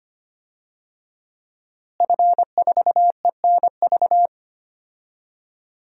Morse code, heard as a keyed tone, about 25 wpm, 700 Hz. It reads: F4ENV
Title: DXCC-100 Call Signs - 5 Characters - Repeated 2x - 25wpm